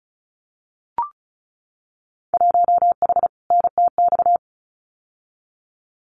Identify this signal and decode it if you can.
Morse code, heard as a keyed tone, about 36 wpm, 690 Hz. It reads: 1H NTX
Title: Winter Field Day Exchanges 35wpm